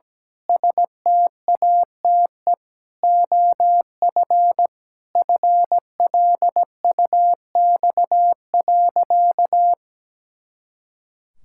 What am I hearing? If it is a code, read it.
Morse code, heard as a keyed tone, about 17 wpm, 700 Hz. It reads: STATE OF FLUX.
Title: Pangram Challenge v2 - 17wpm